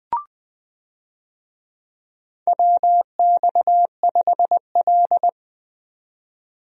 Morse code, heard as a keyed tone, about 20 wpm, 700 Hz. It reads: WX5L
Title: USA Call Signs - 4 Characters - 20wpm